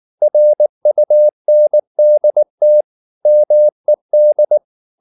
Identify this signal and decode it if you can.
Morse code, heard as a keyed tone, about 19 wpm, 600 Hz. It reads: RUNDT MED
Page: Ferie 19 wpm